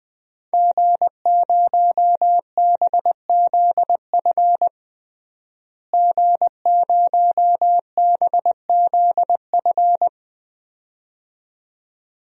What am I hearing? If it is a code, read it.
Morse code, heard as a keyed tone, about 20 wpm, 700 Hz. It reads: G0BZF G0BZF
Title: DXCC-100 Call Signs - 5 Characters - Repeated 2x - 20wpm